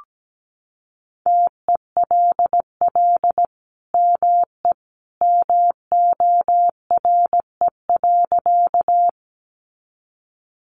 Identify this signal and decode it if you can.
Morse code, heard as a keyed tone, about 17 wpm, 700 Hz. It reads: TELL ME MORE.
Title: Sentences from Top 300 Words 17wpm